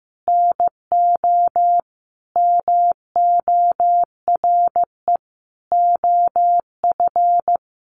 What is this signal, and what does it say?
Morse code, heard as a keyed tone, about 15 wpm, 700 Hz. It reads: NO MORE OF
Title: Sentences from Top 1100 Words 15wpm